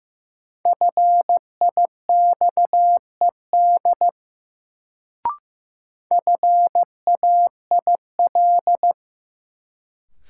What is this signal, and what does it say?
Morse code, heard as a keyed tone, about 15 wpm, 700 Hz. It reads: FIXED FAIL
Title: Sentences from Top 1100 Words - New Word Drill - 15wpm